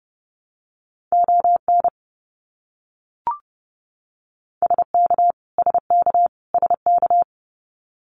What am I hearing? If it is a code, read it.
Morse code, heard as a keyed tone, about 30 wpm, 700 Hz. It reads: ON SK SK SK
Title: Canadian Provinces and Territories - Repeated 3x - 30wpm